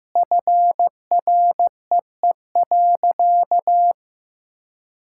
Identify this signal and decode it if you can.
Morse code, heard as a keyed tone, about 15 wpm, 700 Hz. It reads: FREE.
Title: Sentences from Top 1000 Words - Review Part 5 - 15wpm